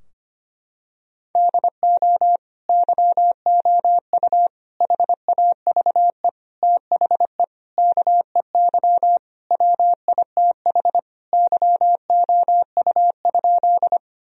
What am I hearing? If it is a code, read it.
Morse code, heard as a keyed tone, about 25 wpm, 700 Hz. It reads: DO YOU HAVE THE KEY WITH YOU?
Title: Sentences from Top 1000 Words - Review Part 4 - 25wpm